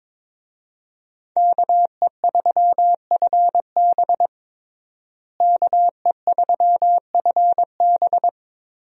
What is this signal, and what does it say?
Morse code, heard as a keyed tone, about 22 wpm, 700 Hz. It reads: KE3FB KE3FB
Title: USA Call Signs - 5 Characters - Repeated 2x - 22wpm